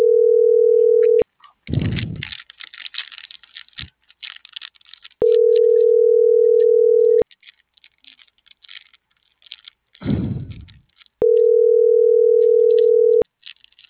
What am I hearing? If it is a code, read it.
Ringback tone (North America), 440 + 480 Hz.